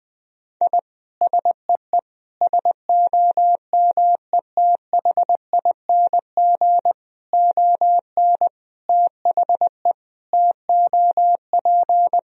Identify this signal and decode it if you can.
Morse code, heard as a keyed tone, about 21 wpm, 700 Hz. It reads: I SEE SOMETHING ON THE TOP
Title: Sentences from Top 1000 Words - Review Part 2 - 20wpm